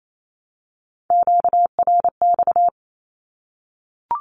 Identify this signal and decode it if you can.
Morse code, heard as a keyed tone, about 28 wpm, 700 Hz. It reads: QRX
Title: Q-Codes Repeated 3x 28wpm